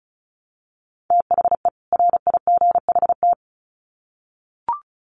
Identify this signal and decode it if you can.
Morse code, heard as a keyed tone, about 34 wpm, 700 Hz. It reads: THE RIGHT
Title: Sets of 2 Words 35wpm